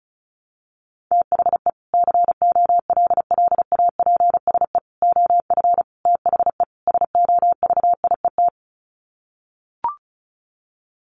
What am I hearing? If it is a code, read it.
Morse code, heard as a keyed tone, about 34 wpm, 700 Hz. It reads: THE COLLAPSE OF THE SOVIET
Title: Sets of 5 Words Encore 35wpm